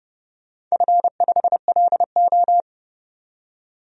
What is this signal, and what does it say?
Morse code, heard as a keyed tone, about 30 wpm, 700 Hz. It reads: F5LO
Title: DXCC-100 Call Signs - 4 Characters - 30wpm